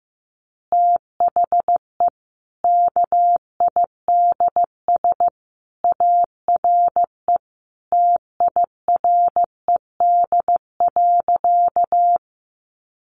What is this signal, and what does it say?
Morse code, heard as a keyed tone, about 14 wpm, 700 Hz. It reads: THE KIDS ARE TIRED.
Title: Easy Phrases 15wpm